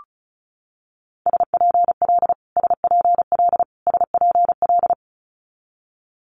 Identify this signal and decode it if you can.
Morse code, heard as a keyed tone, about 36 wpm, 700 Hz. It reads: SPL SPL SPL
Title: Multi-letter Phonemes Repeated 3x 35wpm